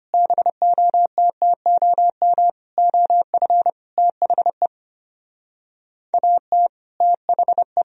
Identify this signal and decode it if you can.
Morse code, heard as a keyed tone, about 31 wpm, 700 Hz. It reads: BOTTOM OF THE AT THE
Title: Speed-Racing - Sets of 5 Words 25wpm